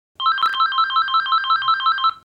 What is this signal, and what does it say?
There is no code.